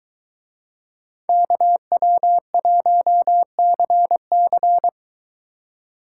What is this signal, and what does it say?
Morse code, heard as a keyed tone, about 22 wpm, 700 Hz. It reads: KW1CC